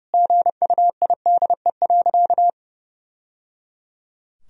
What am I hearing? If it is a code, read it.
Morse code, heard as a keyed tone, about 30 wpm, 700 Hz. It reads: GUIDE.
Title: Sentences from Top 1000 Words - Review Part 3 - 30wpm